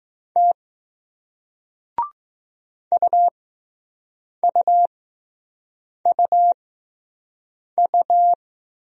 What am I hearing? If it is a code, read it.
Morse code, heard as a keyed tone, about 20 wpm, spaced as for 6 wpm, 700 Hz. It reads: T UUUU